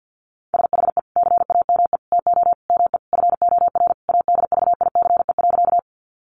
Morse code, heard as a keyed tone, about 50 wpm, 700 Hz. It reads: SHE CAME TO ME FOR ADVICE.
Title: Sentences from Top 1100 Words 50wpm